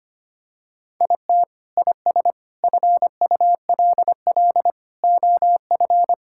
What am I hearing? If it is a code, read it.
Morse code, heard as a keyed tone, about 25 wpm, 700 Hz. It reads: IT IS FULL OF
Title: Sentences from Top 1000 Words - Review Part 5 - 25wpm